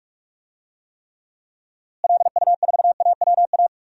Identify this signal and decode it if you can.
Morse code, heard as a keyed tone, about 41 wpm, 700 Hz. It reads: LU4AWA
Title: DXCC-100 Call Signs - 6 Characters - Repeated 3x - 45wpm